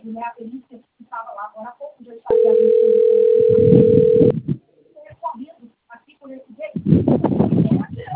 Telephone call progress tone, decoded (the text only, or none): ringback tone